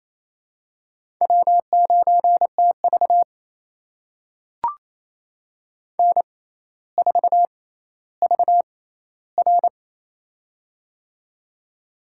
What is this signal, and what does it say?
Morse code, heard as a keyed tone, about 28 wpm, spaced as for 14 wpm, 700 Hz. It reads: W9TV N4VR
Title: USA Call Signs - 4 Characters - 6x Character Spacing - 28wpm